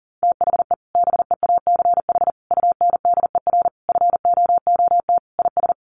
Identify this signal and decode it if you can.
Morse code, heard as a keyed tone, about 40 wpm, 700 Hz. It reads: THE BEACH UNDER FOOT IS
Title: Sentences from Top 1500 Words 40wpm